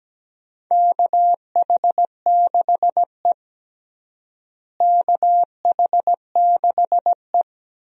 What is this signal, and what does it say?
Morse code, heard as a keyed tone, about 17 wpm, 700 Hz. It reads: KH6E KH6E
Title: DXCC-100 Call Signs - 4 Characters - Repeated 2x - 17wpm